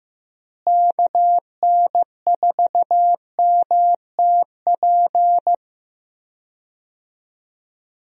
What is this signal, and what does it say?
Morse code, heard as a keyed tone, about 15 wpm, 700 Hz. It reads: KN4MTP
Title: USA Call Signs - 6 Characters - Repeated 3x - 15wpm